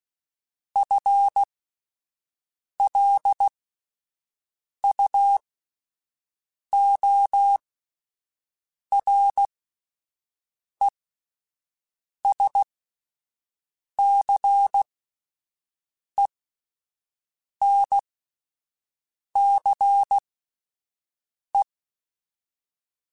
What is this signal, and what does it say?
Morse code, heard as a keyed tone, about 16 wpm, spaced as for 5 wpm, 800 Hz. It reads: FLUORESCENCE
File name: Spaced / FL Runs sp mp3s / FL Runs sp @10